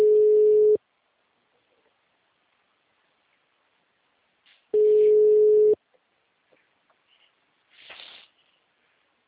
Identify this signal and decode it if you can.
Ringback tone (Europe, 425 Hz).
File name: out- -2017- - - .2